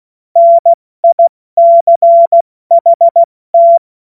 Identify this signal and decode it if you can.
Morse code, heard as a keyed tone, about 16 wpm, spaced as for 14 wpm, 680 Hz. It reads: NICHT
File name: Morsekurs / Witze / Witz 21. Dezember 2021, 70 F80